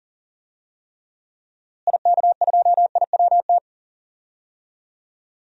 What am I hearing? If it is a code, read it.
Morse code, heard as a keyed tone, about 40 wpm, 700 Hz. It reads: IK2IWT
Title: DXCC-100 Call Signs - 6 Characters - Repeated 3x - 40wpm